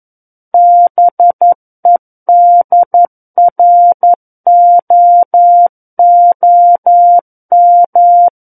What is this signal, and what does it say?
Morse code, heard as a keyed tone, about 11 wpm, 700 Hz. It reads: BEDROOM
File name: Book One / 02-Chapter 2 - 11 WPM